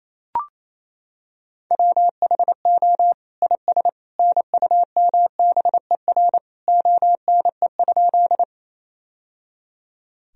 Morse code, heard as a keyed tone, about 28 wpm, 700 Hz. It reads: WHO IS NUMBER ONE?
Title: Sentences from Top 1000 Words - Review Part 1 - 28wpm